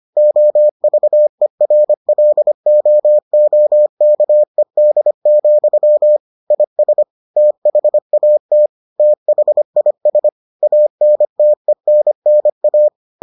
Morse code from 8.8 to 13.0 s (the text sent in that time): THIS ANTENNA